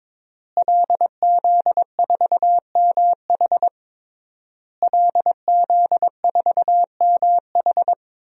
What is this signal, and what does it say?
Morse code, heard as a keyed tone, about 22 wpm, 700 Hz. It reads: LZ4MH LZ4MH
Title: DXCC-100 Call Signs - 5 Characters - Repeated 3x - 22wpm